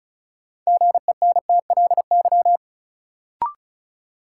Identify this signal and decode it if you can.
Morse code, heard as a keyed tone, about 35 wpm, 700 Hz. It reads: GENTLY
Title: Sentences from Top 1500 Words - New Word Drill - 35wpm